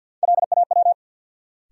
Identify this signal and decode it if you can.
Morse code, heard as a keyed tone, about 50 wpm, 700 Hz. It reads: LAW
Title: Sentences from Top 600 Words - New Word Drill - 50wpm